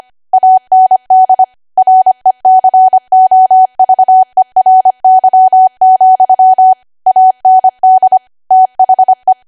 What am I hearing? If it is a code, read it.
Morse code, heard as a keyed tone, about 25 wpm, 750 Hz. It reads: AND RECOVERY, AND THE
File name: Archive / 25 WPM / 25WPM